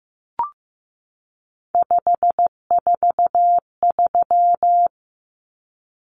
Morse code, heard as a keyed tone, about 15 wpm, 700 Hz. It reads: 543
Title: Numbers - 3-Digit 15wpm